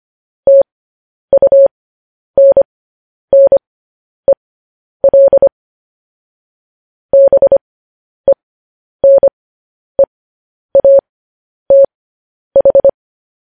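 Morse code, heard as a keyed tone, about 25 wpm, spaced as for 10 wpm, 550 Hz. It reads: TUNNEL BENEATH
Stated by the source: Morse code news headlines practice transmission.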